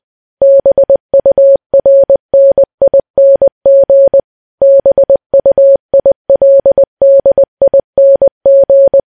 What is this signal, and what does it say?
Morse code, heard as a keyed tone, about 20 wpm, 550 Hz. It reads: BURNING BUILDING
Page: news-headlines-in-morse-code-at-20-wpm